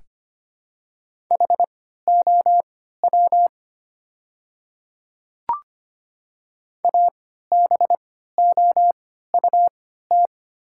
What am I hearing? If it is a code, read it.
Morse code, heard as a keyed tone, about 25 wpm, spaced as for 14 wpm, 700 Hz. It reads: HOW ABOUT
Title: 3x Character Spacing - Top 100 Words - 25wpm